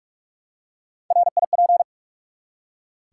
Morse code, heard as a keyed tone, about 48 wpm, 700 Hz. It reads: RIP
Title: Three Letter Words - 45wpm